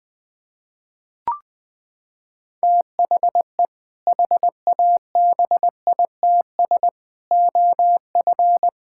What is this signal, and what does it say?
Morse code, heard as a keyed tone, about 19 wpm, 700 Hz. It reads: THE HABITS OF
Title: Sentences with 2-Word Phrases - 20wpm